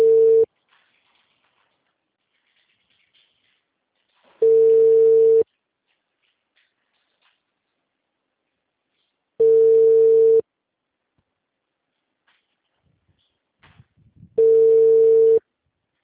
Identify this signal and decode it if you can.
Ringback tone (China, 450 Hz).
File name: out- -2017- - - .30196